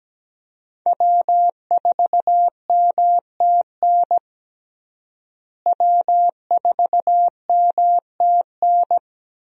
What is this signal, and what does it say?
Morse code, heard as a keyed tone, about 17 wpm, 700 Hz. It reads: W4MTN W4MTN
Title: USA Call Signs - 5 Characters - Repeated 2x - 17wpm